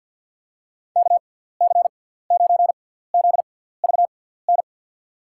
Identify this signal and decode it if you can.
Morse code, heard as a keyed tone, about 48 wpm, spaced as for 17 wpm, 700 Hz. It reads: KC9ZVN
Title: USA Call Signs - 6 Characters - 6x Character Spacing - 50wpm